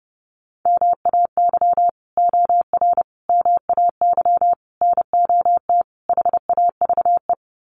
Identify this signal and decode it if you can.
Morse code, heard as a keyed tone, about 30 wpm, 700 Hz. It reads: MAY OR MAY NOT HAVE
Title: Sets of 5 Words Encore2 30wpm